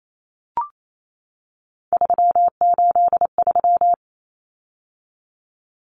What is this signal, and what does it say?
Morse code, heard as a keyed tone, about 28 wpm, 700 Hz. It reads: 383
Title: Numbers - 3-Digit 28wpm